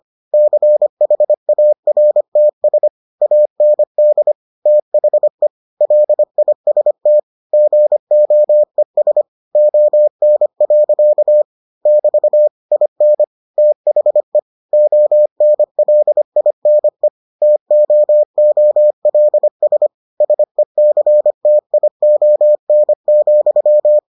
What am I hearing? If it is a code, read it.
Morse code, heard as a keyed tone, about 25 wpm, 600 Hz. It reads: CHARTS AND THE LIST GOES ON. <BT> IN THE ONLINE TOOLS SECTION,